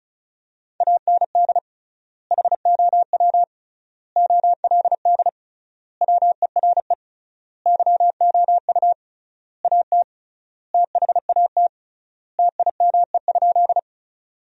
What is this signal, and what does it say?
Morse code, heard as a keyed tone, about 36 wpm, 700 Hz. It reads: AND HOW OLD WERE YOU AT THAT TIME?
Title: 3x Word-Spacing - Sentences from Top 200 Words - 35wpm